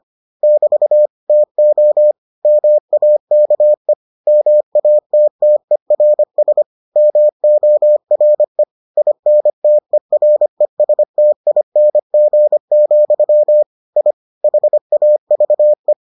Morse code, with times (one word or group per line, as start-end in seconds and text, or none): 0.4-1.1 <BT>
1.3-2.1 TO
2.5-3.9 MAKE
4.3-6.6 MATTERS
7.0-8.6 MORE
9.0-13.6 INTERESTING,
14.0-14.1 I
14.4-15.9 HAVE